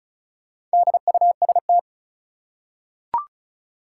Morse code, heard as a keyed tone, about 36 wpm, 700 Hz. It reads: DUST
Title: Four Letter Words - 35wpm